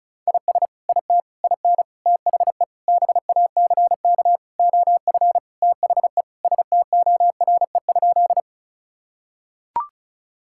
Morse code, heard as a keyed tone, about 34 wpm, 700 Hz. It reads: IS IT IN THE BACK OF THE STORE?
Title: Sentences from Top 1000 Words - Review Part 5 - 35wpm